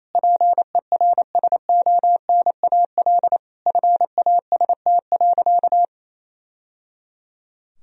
Morse code, heard as a keyed tone, about 28 wpm, 700 Hz. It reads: PERSONAL FAST.
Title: Sentences from Top 1000 Words - Review Part 4 - 28wpm